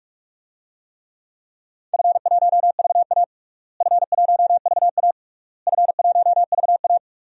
Morse code, heard as a keyed tone, about 43 wpm, 700 Hz. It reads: F1VA F1VA F1VA
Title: DXCC-100 Call Signs - 4 Characters - Repeated 3x - 45wpm